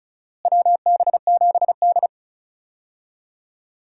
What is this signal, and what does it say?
Morse code, heard as a keyed tone, about 36 wpm, 700 Hz. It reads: WB7D
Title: USA Call Signs - 4 Characters - Repeated 3x - 35wpm